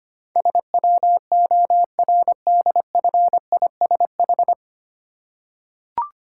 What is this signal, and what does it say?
Morse code, heard as a keyed tone, about 25 wpm, 700 Hz. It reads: SWORDFISH
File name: practice / 25wpm / Difficult Words with Hints - 25wpm